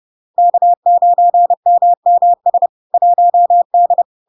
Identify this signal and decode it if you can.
Morse code, heard as a keyed tone, about 30 wpm, 700 Hz. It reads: K9MMS 1D
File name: LARK-FD-2022-Log-in-CW-30wpm-700-Hz